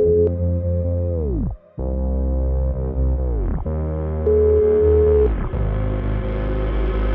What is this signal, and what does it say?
Ringback tone (Europe, 425 Hz), with noise.